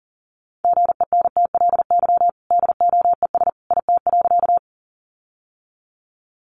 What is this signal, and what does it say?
Morse code, heard as a keyed tone, about 40 wpm, 700 Hz. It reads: GENTLY DOES IT.
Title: Sentences from Top 1500 Words 40wpm